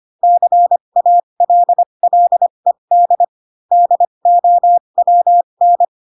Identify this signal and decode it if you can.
Morse code, heard as a keyed tone, about 25 wpm, spaced as for 22 wpm, 700 Hz. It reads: CALLED DOWN
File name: Pinocchio / Pinocchio ch6 18 25 ik1hge